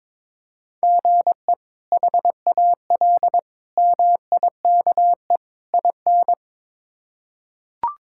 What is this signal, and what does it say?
Morse code, heard as a keyed tone, about 22 wpm, 700 Hz. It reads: GE HAL MIKE IN
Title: K1USN SST QSO Practice - 22wpm